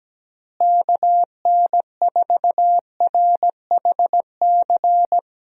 Morse code, heard as a keyed tone, about 17 wpm, 700 Hz. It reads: KN4RHC
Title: USA Call Signs - 6 Characters - Repeated 3x - 17wpm